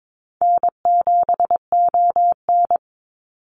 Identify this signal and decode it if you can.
Morse code, heard as a keyed tone, about 22 wpm, 700 Hz. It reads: N7ON
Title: Easy Call Signs - 22wpm